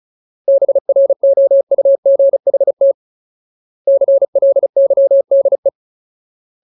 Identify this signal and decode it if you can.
Morse code, heard as a keyed tone, about 35 wpm, 550 Hz. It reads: BROUGHT CLYDE
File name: Ultimate Word List / 35WPM Ultimate